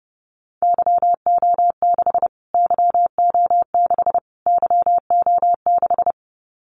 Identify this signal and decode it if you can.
Morse code, heard as a keyed tone, about 30 wpm, 700 Hz. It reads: YO6 YO6 YO6
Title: Callsign Prefixes for Top 25 DXCC - Repeated 3x - 30wpm